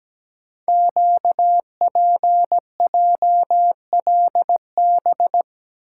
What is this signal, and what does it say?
Morse code, heard as a keyed tone, about 17 wpm, 700 Hz. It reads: QPJLB
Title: Code Groups - 5 Letters - 17wpm